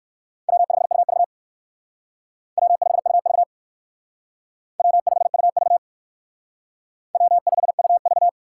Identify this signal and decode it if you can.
Morse code, heard as a keyed tone, about 56 wpm, 700 Hz. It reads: W5UV W5UV W5UV W5UV